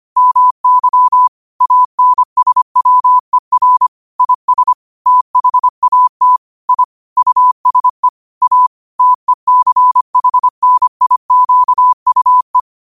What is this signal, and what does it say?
Morse code, heard as a keyed tone, about 25 wpm, 1000 Hz. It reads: MY ANSWER IS THAT I USE A TECHNIQUE